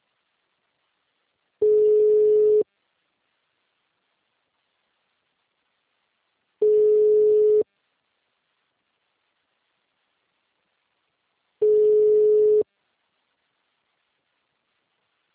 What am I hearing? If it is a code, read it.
Ringback tone (Europe, 425 Hz).